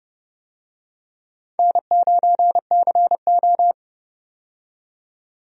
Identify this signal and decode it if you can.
Morse code, heard as a keyed tone, about 31 wpm, 700 Hz. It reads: N9CO
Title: USA Call Signs - 4 Characters - 30wpm